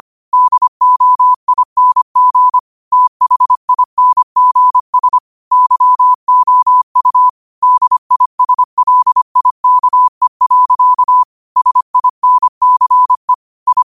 Morse code, heard as a keyed tone, about 25 wpm, 1000 Hz. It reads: DOING THINGS YOU DISLIKE. SINCE I
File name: qotdfiles / 25wpm